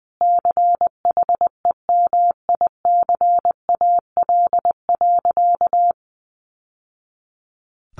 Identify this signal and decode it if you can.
Morse code, heard as a keyed tone, about 20 wpm, 700 Hz. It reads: CHEMICAL.
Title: Sentences from Top 1200 Words 20wpm